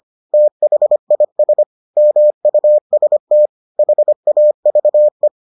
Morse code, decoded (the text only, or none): THIS MUST HAVE